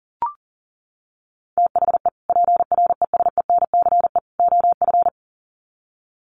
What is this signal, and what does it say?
Morse code, heard as a keyed tone, about 40 wpm, 700 Hz. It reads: THE PRESENCE OF
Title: Sets of 3 Words 40wpm